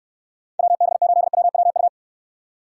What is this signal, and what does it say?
Morse code, heard as a keyed tone, about 66 wpm, 700 Hz. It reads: WB8WWU